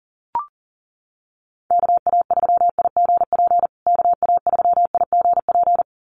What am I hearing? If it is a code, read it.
Morse code, heard as a keyed tone, about 40 wpm, 700 Hz. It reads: KA3IGP KA3IGP
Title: USA Call Signs Repeated 2x 40wpm